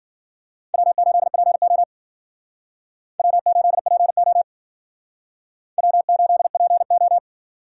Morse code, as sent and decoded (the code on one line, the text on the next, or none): .-- ---.. .--. --- / .-- ---.. .--. --- / .-- ---.. .--. ---
W8PO W8PO W8PO